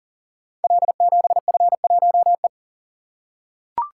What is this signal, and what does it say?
Morse code, heard as a keyed tone, about 42 wpm, 690 Hz. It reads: L7F1E
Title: Code Groups - 5 Letter:Numbers - 40wpm